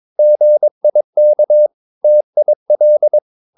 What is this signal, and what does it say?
Morse code, heard as a keyed tone, about 22 wpm, 600 Hz. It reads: GIK TIL
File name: Ferie QRS 21wpm